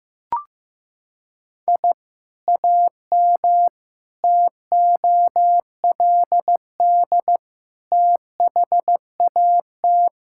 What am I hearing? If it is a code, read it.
Morse code, heard as a keyed tone, about 15 wpm, 700 Hz. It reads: I AM TOLD THAT
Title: Sentences from Top 1000 Words - Review Part 5 - 15wpm